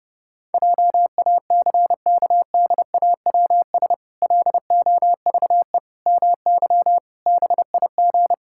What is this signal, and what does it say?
Morse code, heard as a keyed tone, about 30 wpm, 700 Hz. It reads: JACKDAWS LOVE MY BIG
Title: Pangram Challenge v2 - 30wpm